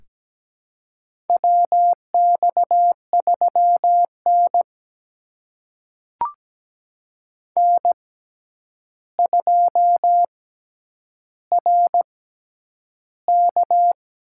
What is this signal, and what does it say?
Morse code, heard as a keyed tone, about 17 wpm, spaced as for 9 wpm, 700 Hz. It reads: WX3N N2RK
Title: USA Call Signs - 4 Characters - 6x Character Spacing - 17wpm